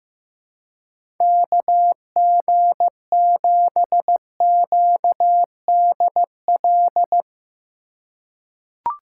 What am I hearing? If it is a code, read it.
Morse code, heard as a keyed tone, about 15 wpm, 700 Hz. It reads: KG7QDL
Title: USA Call Signs - 6 Characters - 4x Character Spacing - 15wpm